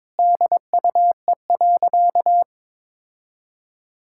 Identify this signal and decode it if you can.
Morse code, heard as a keyed tone, about 22 wpm, 700 Hz. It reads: DUE.
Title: Sentences from Top 1000 Words - Review Part 6 - 22wpm